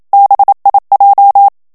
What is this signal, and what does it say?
Morse code, heard as a keyed tone, about 27 wpm, 800 Hz. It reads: BIJ
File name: HOLLANDAIS texte clair 1 vitesse 26 mots minute